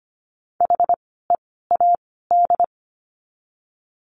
Morse code, heard as a keyed tone, about 25 wpm, 700 Hz. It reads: H E A D
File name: course / 25:10wpm / 094 Lesson 16f - Letter c - Words 25wpm